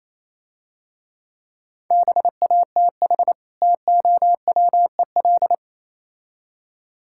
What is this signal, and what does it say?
Morse code, heard as a keyed tone, about 27 wpm, 700 Hz. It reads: BATH TOWEL
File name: Easy Words - Speed-Racing with Hints - 22wpm